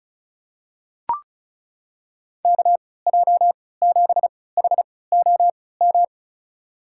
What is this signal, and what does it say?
Morse code, heard as a keyed tone, about 36 wpm, spaced as for 20 wpm, 700 Hz. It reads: KJ7HOM
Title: USA Call Signs - 3x Character Spacing - 35wpm